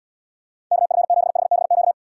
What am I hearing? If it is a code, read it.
Morse code, heard as a keyed tone, about 78 wpm, 700 Hz. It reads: KF7HBQ